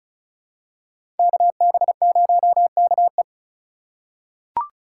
Morse code, heard as a keyed tone, about 34 wpm, 700 Hz. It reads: KB0KE
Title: USA Call Signs - 5 Characters - 2x Character Spacing - 35wpm